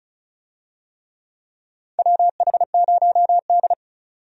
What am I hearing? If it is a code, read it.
Morse code, heard as a keyed tone, about 34 wpm, 700 Hz. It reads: WH0D
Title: USA Call Signs - 4 Characters - 35wpm